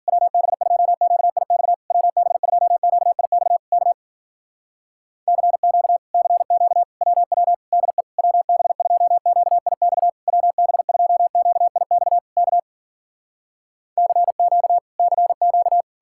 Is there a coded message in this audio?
Yes: Morse code from 0.1 to 15.8 s, heard as a keyed tone, about 60 wpm, 700 Hz. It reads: WB2QIX WB2QIX K CQ CQ WW DE WB2QIX WB2QIX K CQ CQ